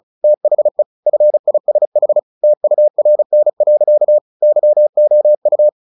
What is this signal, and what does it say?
Morse code, heard as a keyed tone, about 34 wpm, 600 Hz. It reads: THE FISH TURN. YOU
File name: the old man and the sea 2 35wpm